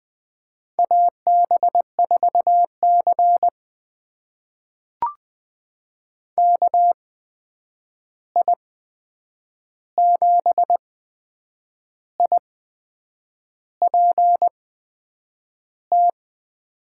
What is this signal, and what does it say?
Morse code, heard as a keyed tone, about 20 wpm, spaced as for 8 wpm, 700 Hz. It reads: AB4C KI7IPT
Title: USA Call Signs - 8x Character Spacing - 20wpm